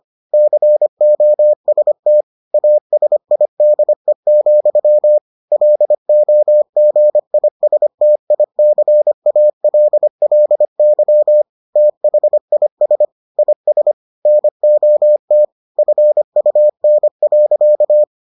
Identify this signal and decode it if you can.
Morse code, heard as a keyed tone, about 25 wpm, 600 Hz. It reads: COST ASIDE, LOGISTICALLY THIS IS NOT FUN.